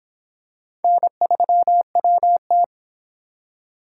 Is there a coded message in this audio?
Yes: Morse code from 0.8 to 2.6 s, heard as a keyed tone, about 25 wpm, 700 Hz. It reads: N3WT